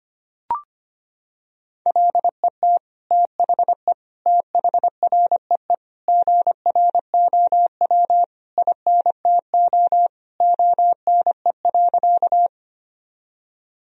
Morse code, heard as a keyed tone, about 25 wpm, 700 Hz. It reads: LET THE THREE GROW INTO ONE.
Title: Sentences from Top 1000 Words - Review Part 1 - 25wpm